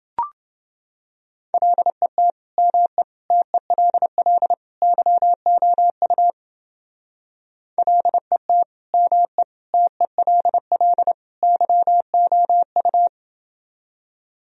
Morse code, heard as a keyed tone, about 28 wpm, 700 Hz. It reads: LET ME TELL YOU LET ME TELL YOU